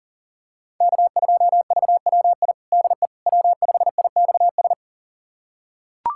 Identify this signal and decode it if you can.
Morse code, heard as a keyed tone, about 40 wpm, 700 Hz. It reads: K2VWI DE W5IXS
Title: USA Callsign DE Callsign - 5 Characters - 40wpm